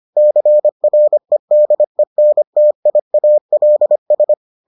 Morse code, heard as a keyed tone, about 25 wpm, 600 Hz. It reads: CREDENTIALS